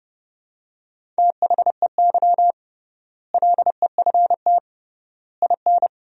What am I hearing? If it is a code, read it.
Morse code, heard as a keyed tone, about 30 wpm, 700 Hz. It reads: THEY LEFT IN
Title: 3x Word-Spacing - Sentences from Top 400 Words - 30wpm